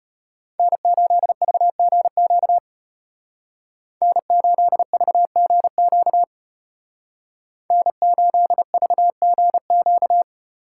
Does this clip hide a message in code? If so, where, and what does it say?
Yes: Morse code from 0.6 to 10.2 s, heard as a keyed tone, about 33 wpm, 700 Hz. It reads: N8VGQ N8VGQ N8VGQ